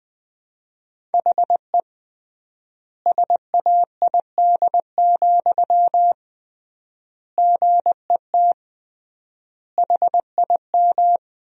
Morse code, heard as a keyed tone, about 21 wpm, 700 Hz. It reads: HE SAID, GET HIM
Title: 3x Word-Spacing - Sentences from Top 400 Words - 20wpm